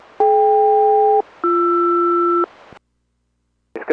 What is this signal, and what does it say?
Aircraft SELCAL, heard as two pairs of tones. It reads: DK-BR